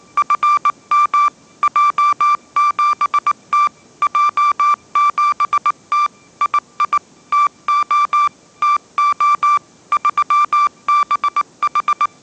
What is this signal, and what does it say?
Morse code, heard as a keyed tone, about 19 wpm, 1180 Hz. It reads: FM J7T J7T II TO TO 3BH
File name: mrx083 20wpm